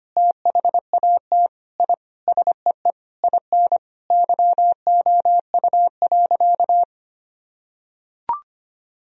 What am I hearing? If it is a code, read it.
Morse code, heard as a keyed tone, about 25 wpm, 700 Hz. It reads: THAT I SEE IN YOU.
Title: Sentences from Top 1000 Words - Review Part 2 - 25wpm